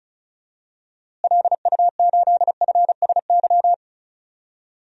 Morse code, heard as a keyed tone, about 34 wpm, 700 Hz. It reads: LU8FSY
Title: DXCC-100 Call Signs - 6 Characters - Repeated 2x - 35wpm